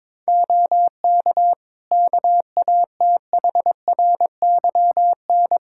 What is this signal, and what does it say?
Morse code, heard as a keyed tone, about 22 wpm, 700 Hz. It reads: OK KATHRYN
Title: Practice QSO - Exchange 4 - Ending - 22wpm